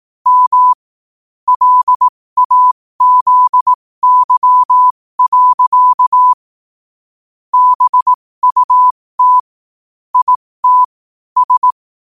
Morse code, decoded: M LAZY. BUT IT S

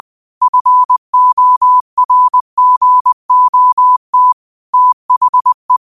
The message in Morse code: FORGOT THE

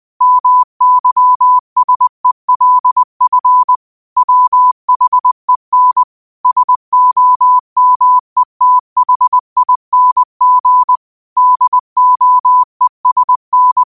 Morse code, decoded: MYSELF WHEN SOMETHING DOESN

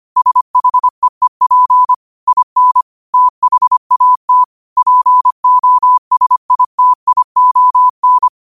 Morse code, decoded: SHEEP IN THAT POSITION

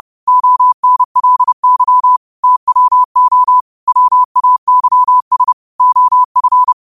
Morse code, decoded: ONLY TWO WAYS OF

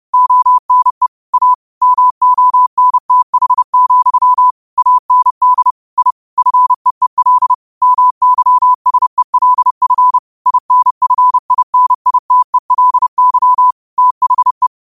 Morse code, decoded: ONE A MONTH, AND I FEEL MYSELF INFINITELY THE

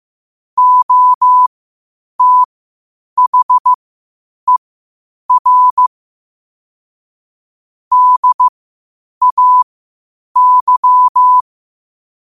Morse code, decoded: OTHER DAY

